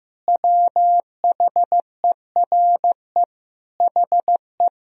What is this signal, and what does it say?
Morse code, heard as a keyed tone, about 15 wpm, 700 Hz. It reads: WHERE HE
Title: Sentences from Top 1000 Words - Review Part 1 - 15wpm